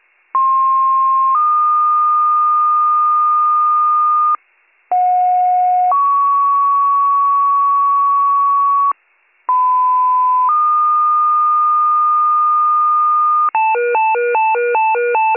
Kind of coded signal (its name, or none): two-tone page